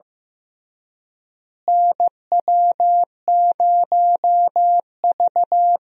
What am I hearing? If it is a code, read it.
Morse code, heard as a keyed tone, about 15 wpm, 700 Hz. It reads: NW0V
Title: USA Call Signs - 4 Characters - Repeated 2x - 15wpm